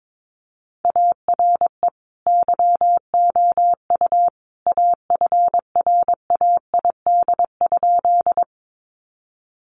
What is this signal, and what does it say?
Morse code, heard as a keyed tone, about 22 wpm, 700 Hz. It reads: ARE YOU AFRAID?
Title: Sentences from Top 1500 Words 22wpm